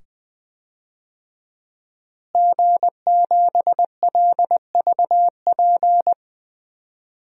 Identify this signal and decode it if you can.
Morse code, heard as a keyed tone, about 21 wpm, 700 Hz. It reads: G7LVP
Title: DXCC-100 Call Signs - 5 Characters - Repeated 2x - 20wpm